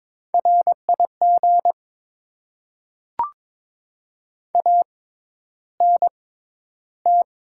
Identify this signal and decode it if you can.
Morse code, heard as a keyed tone, about 22 wpm, spaced as for 11 wpm, 700 Hz. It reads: RIG ANT
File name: Top 20 QSO Elements - 6x Character Spacing - 22wpm